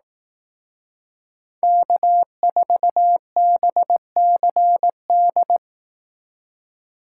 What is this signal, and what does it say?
Morse code, heard as a keyed tone, about 19 wpm, 700 Hz. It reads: K4BCD
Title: Speed Racing - USA Call Signs 15wpm